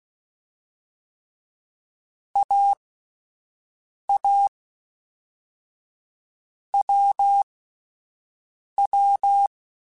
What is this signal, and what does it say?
Morse code, heard as a keyed tone, about 16 wpm, 800 Hz. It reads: A A W W
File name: Short Runs / Short Runs-mp3s / awj1ndb6gz7 @10